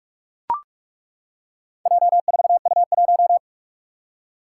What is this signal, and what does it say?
Morse code, heard as a keyed tone, about 42 wpm, 700 Hz. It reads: J4U1
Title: Four Challenging Letter-Number Combinations 45wpm